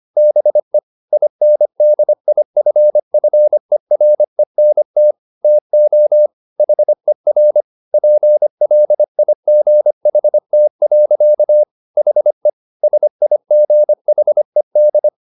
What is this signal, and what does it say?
Morse code, heard as a keyed tone, about 25 wpm, 600 Hz. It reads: BE INDIFFERENT TO HER PLIGHT. HE SIGHED